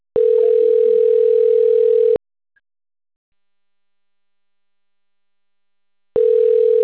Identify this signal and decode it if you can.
Ringback tone (North America), 440 + 480 Hz.
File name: rg-1600-unknown- - - .60691